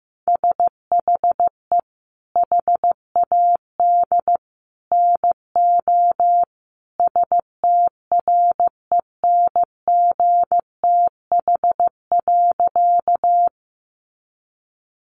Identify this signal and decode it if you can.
Morse code, heard as a keyed tone, about 14 wpm, 700 Hz. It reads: SHE HAD NO STRENGTH.
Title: Sentences from Top 1300 Words 15wpm